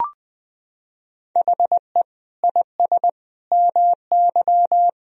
Morse code, heard as a keyed tone, about 21 wpm, 700 Hz. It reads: HE IS MY
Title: Sentences from Top 1000 Words - Review Part 2 - 20wpm